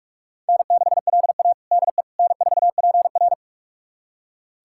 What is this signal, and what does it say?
Morse code, heard as a keyed tone, about 42 wpm, 700 Hz. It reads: N6LA DE N4PR
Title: USA Callsign DE Callsign - 4 Characters - 45wpm